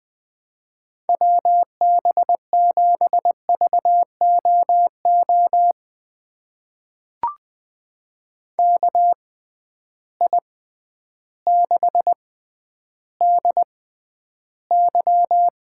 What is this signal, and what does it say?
Morse code, heard as a keyed tone, about 20 wpm, spaced as for 11 wpm, 700 Hz. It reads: WB7VOO KI6DY
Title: USA Call Signs - 6 Characters - 6x Character Spacing - 20wpm